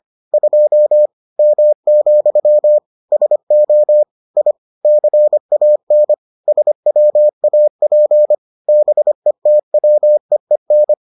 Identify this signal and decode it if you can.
Morse code, heard as a keyed tone, about 25 wpm, 600 Hz. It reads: 2 M, SO I CAN SWAP BETWEEN